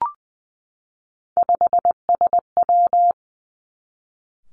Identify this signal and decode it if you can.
Morse code, heard as a keyed tone, about 20 wpm, 700 Hz. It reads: 5SW
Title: Three Challenging Letter-Number Combinations 20wpm